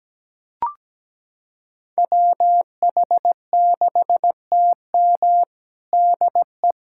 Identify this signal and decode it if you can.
Morse code, heard as a keyed tone, about 17 wpm, 700 Hz. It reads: WH6TM DE
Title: USA Callsign DE Callsign - 5 Characters - 17wpm